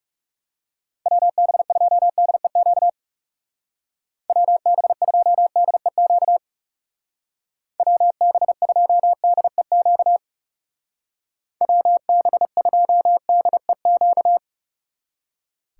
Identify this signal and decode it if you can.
Morse code, heard as a keyed tone, about 37 wpm, 700 Hz. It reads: WB2DEQ WB2DEQ WB2DEQ WB2DEQ
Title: Speed Racing - USA 2x3 Call Signs 30wpm